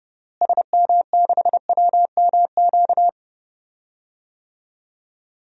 Morse code, heard as a keyed tone, about 31 wpm, 700 Hz. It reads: SM6WMQ
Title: DXCC-100 Call Signs - 6 Characters - Repeated 3x - 30wpm